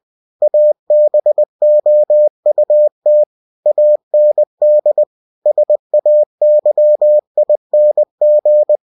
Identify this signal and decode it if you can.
Morse code, heard as a keyed tone, about 20 wpm, 600 Hz. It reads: ABOUT AND SAYING
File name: cw-books / winnie-the-pooh / winnie-20wpm-page-0066